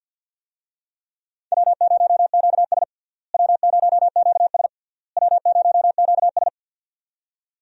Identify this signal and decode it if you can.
Morse code, heard as a keyed tone, about 52 wpm, 700 Hz. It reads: W0QS W0QS W0QS
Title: USA Call Signs - 4 Characters - Repeated 3x - 50wpm